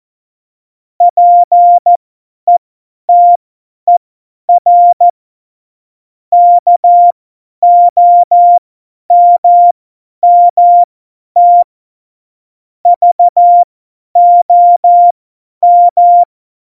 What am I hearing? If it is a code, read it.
Morse code, heard as a keyed tone, about 14 wpm, spaced as for 10 wpm, 700 Hz. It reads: PETER KOMMT VOM
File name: Morsekurs / Witze / Witz 26. Dezember 2017, 50 F70